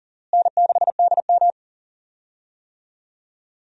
Morse code, heard as a keyed tone, about 40 wpm, 700 Hz. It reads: N6DM
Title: USA Call Signs - 4 Characters - Repeated 3x - 40wpm